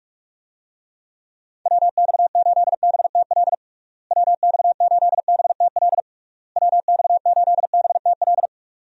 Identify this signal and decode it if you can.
Morse code, heard as a keyed tone, about 46 wpm, 700 Hz. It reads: WX8BTL WX8BTL WX8BTL
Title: USA Call Signs - 6 Characters - Repeated 3x - 45wpm